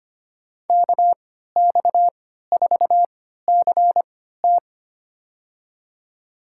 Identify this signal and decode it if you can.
Morse code, heard as a keyed tone, about 25 wpm, spaced as for 14 wpm, 700 Hz. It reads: KX4CT
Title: USA Call Signs - 5 Characters - 3x Character Spacing - 25wpm